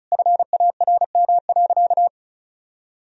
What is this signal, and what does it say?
Morse code, heard as a keyed tone, about 36 wpm, 700 Hz. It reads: FARM.
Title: Sentences from Top 1000 Words - Review Part 2 - 35wpm